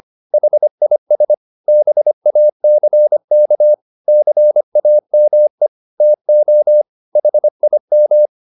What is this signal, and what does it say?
Morse code, heard as a keyed tone, about 25 wpm, 600 Hz. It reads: HIS BACK CAME TO HIM
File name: the old man and the sea part 1 25wpm